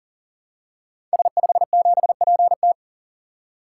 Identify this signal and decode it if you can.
Morse code, heard as a keyed tone, about 40 wpm, 690 Hz. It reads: S57PT
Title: Code Groups - 5 Letter:Numbers - 40wpm